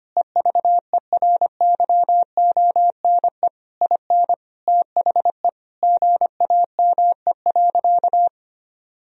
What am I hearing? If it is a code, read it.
Morse code, heard as a keyed tone, about 25 wpm, 700 Hz. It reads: EVERYONE IN THE GAME.
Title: Sentences from Top 1000 Words - Review Part 4 - 25wpm